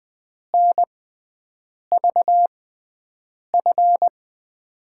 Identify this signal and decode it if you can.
Morse code, heard as a keyed tone, about 21 wpm, spaced as for 7 wpm, 700 Hz. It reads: NVF
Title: USA Call Signs - 6x Character Spacing - 20wpm